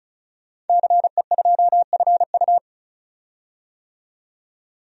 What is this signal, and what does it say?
Morse code, heard as a keyed tone, about 36 wpm, 700 Hz. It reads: CE2FU
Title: DXCC-100 Call Signs - 5 Characters - 35wpm